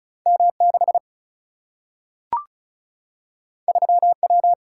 Morse code, heard as a keyed tone, about 36 wpm, 700 Hz. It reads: M 6 3 W
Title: Single-Letter Single-Number 35wpm